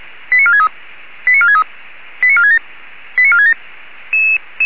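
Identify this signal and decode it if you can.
Five-tone selective call (ZVEI), heard as quick runs of five tones, tones of about 70 ms. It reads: 89463 89463 89467 89467
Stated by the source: ZVEI five-tone call received by SDR.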